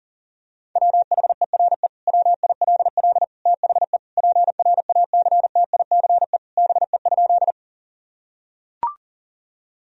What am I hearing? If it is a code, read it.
Morse code, heard as a keyed tone, about 37 wpm, 700 Hz. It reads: WHERE WILL THE PRACTICE BE?
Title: Sentences from Top 1000 Words - Review Part 4 - 40wpm